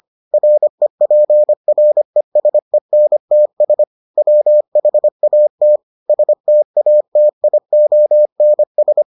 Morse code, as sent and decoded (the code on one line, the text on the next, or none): .-. .--. .-. ... -. - ... / .-- .... .- - / ... - .- - .. --- -. ...
REPRESENTS WHAT STATIONS